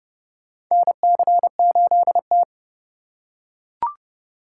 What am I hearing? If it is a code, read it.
Morse code, heard as a keyed tone, about 30 wpm, 700 Hz. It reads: NC8T
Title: USA Call Signs - 4 Characters - 8x Character Spacing - 30wpm